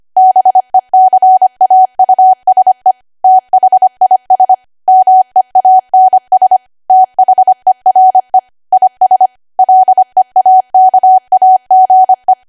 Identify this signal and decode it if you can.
Morse code, heard as a keyed tone, about 25 wpm, 750 Hz. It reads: BECAUSE THIS MEANS THERE IS LEAKAGE
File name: cwa / W1AW-Level3 / 25WPM